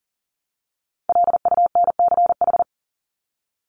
Morse code, heard as a keyed tone, about 40 wpm, 700 Hz. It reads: LUNCH
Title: Top 1751-2000 Words 40wpm